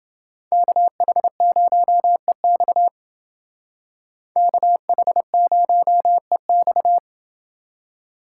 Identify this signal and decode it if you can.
Morse code, heard as a keyed tone, about 29 wpm, 700 Hz. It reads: KH0EX KH0EX